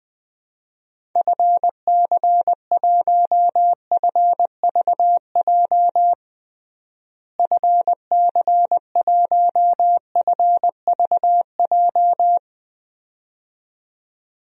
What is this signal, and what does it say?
Morse code, heard as a keyed tone, about 20 wpm, 700 Hz. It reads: FC1FVJ FC1FVJ
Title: DXCC-100 Call Signs - 6 Characters - Repeated 2x - 20wpm